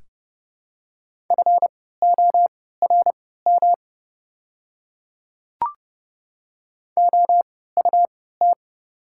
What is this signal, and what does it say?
Morse code, heard as a keyed tone, about 30 wpm, spaced as for 17 wpm, 700 Hz. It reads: FORM OUT
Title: 3x Character Spacing - Top 300 Words - 30wpm